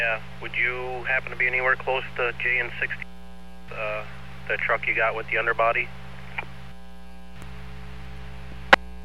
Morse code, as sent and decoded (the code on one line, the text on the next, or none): none